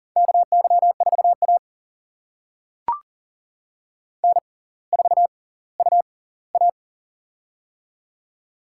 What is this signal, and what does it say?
Morse code, heard as a keyed tone, about 40 wpm, spaced as for 20 wpm, 700 Hz. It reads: KY4A N4UA
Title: USA Call Signs - 4 Characters - 6x Character Spacing - 40wpm